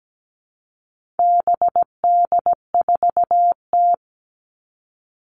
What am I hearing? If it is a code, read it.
Morse code, heard as a keyed tone, about 17 wpm, 700 Hz. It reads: BD4T
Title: All Cut Numbers - 4-Digit 17wpm